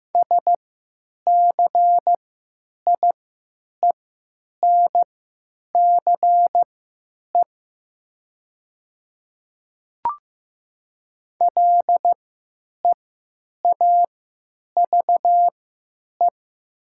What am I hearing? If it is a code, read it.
Morse code, heard as a keyed tone, about 15 wpm, spaced as for 9 wpm, 700 Hz. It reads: SCIENCE LEAVE
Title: 3x Character Spacing - Top 500 Words - 15wpm